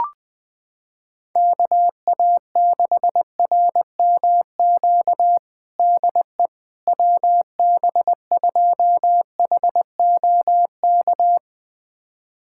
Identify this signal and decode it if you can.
Morse code, heard as a keyed tone, about 20 wpm, 700 Hz. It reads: KA6RMQ DE WB2HOK
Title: USA Callsign DE Callsign - 6 Characters - 20wpm